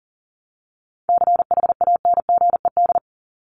Morse code, heard as a keyed tone, about 40 wpm, 700 Hz. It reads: CHANGED
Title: ED Suffix - Top 100 Words with Base Words 40wpm